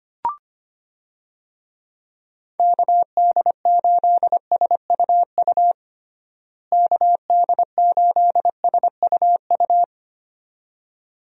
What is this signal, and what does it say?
Morse code, heard as a keyed tone, about 25 wpm, 700 Hz. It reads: KD8SUU KD8SUU
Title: USA Call Signs - 6 Characters - Repeated 2x - 25wpm